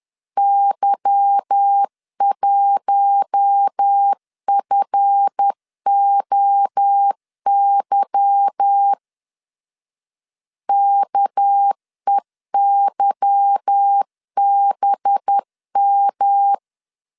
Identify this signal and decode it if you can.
Morse code, heard as a keyed tone, about 10 wpm, 800 Hz. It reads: Y1FOY KEYBM
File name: LEZIONI / 8 YJ45 / 8 Y 10wpm 2615